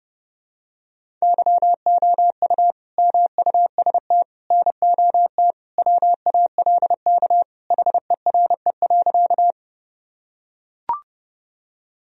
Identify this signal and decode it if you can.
Morse code, heard as a keyed tone, about 30 wpm, 700 Hz. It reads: YOU MUST NOT WALK HERE.
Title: Sentences from Top 1000 Words - Review Part 1 - 30wpm